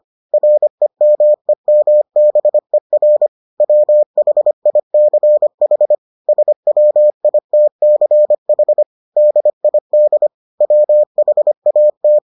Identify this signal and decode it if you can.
Morse code, heard as a keyed tone, about 25 wpm, 600 Hz. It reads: REMEMBER WHICH SWITCH DID WHAT